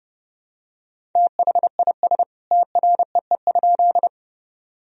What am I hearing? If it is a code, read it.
Morse code, heard as a keyed tone, about 30 wpm, 700 Hz. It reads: THIS TREE?
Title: Sentences from Top 1000 Words - Review Part 1 - 30wpm